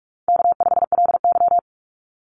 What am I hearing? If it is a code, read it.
Morse code, heard as a keyed tone, about 44 wpm, 700 Hz. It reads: K5LY
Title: Call Signs Repeated 2x - 45wpm